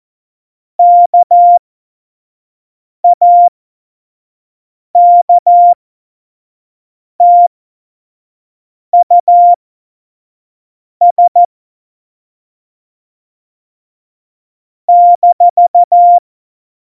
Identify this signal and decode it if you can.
Morse code, heard as a keyed tone, about 14 wpm, spaced as for 5 wpm, 700 Hz. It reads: KAKTUS -
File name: Morsekurs / Witze / Witz 26. September 17, Tempo 25